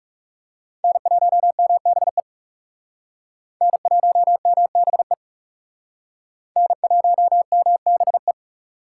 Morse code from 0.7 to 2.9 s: N1MBE

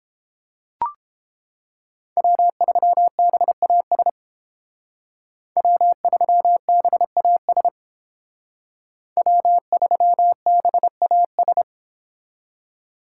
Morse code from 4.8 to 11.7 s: W3BAS W3BAS

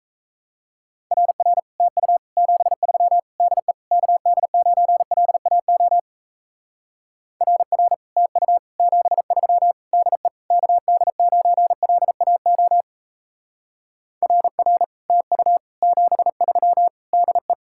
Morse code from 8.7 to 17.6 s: 73 DE KD9LAO RR TU 73 DE

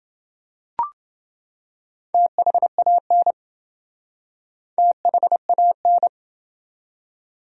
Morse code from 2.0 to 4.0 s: THAN

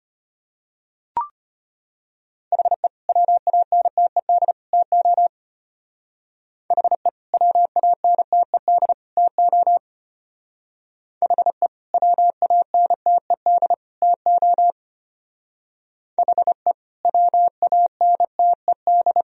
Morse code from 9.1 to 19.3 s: TO HE WANTED TO HE WANTED

